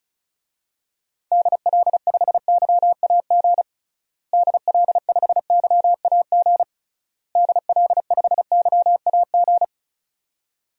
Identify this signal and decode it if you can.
Morse code, heard as a keyed tone, about 36 wpm, 700 Hz. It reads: DL5YAG DL5YAG DL5YAG
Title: DXCC-100 Call Signs - 6 Characters - Repeated 3x - 35wpm